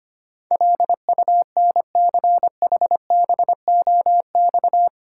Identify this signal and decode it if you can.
Morse code, heard as a keyed tone, about 25 wpm, 690 Hz. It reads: LUNCHBOX
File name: practice / 25wpm / Easy Words with Hints - 25wpm